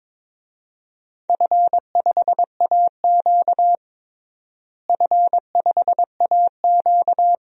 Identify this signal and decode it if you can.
Morse code, heard as a keyed tone, about 21 wpm, 700 Hz. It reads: F5AQ F5AQ
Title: DXCC-100 Call Signs - 4 Characters - Repeated 2x - 22wpm